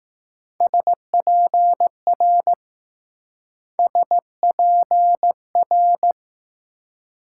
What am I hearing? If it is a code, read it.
Morse code, heard as a keyed tone, about 20 wpm, 700 Hz. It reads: SPR SPR